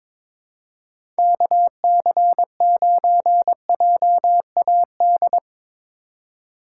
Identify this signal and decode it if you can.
Morse code, heard as a keyed tone, about 22 wpm, 700 Hz. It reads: KC9JAD
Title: USA Call Signs - 6 Characters - 2x Character Spacing - 22wpm